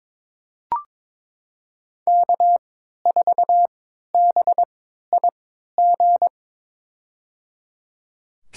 Morse code, heard as a keyed tone, about 22 wpm, spaced as for 13 wpm, 700 Hz. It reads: K4BIG
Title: USA Call Signs - 5 Characters - 3x Character Spacing - 22wpm